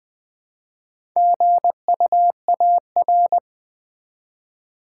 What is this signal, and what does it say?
Morse code, heard as a keyed tone, about 20 wpm, 700 Hz. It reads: GUAR
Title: Build A Word - 1 Letter At A Time - 5-Letter Words - 20wpm